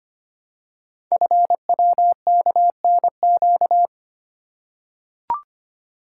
Morse code, heard as a keyed tone, about 25 wpm, 700 Hz. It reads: FWKNQ
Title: Code Groups - 5 Letters - 25wpm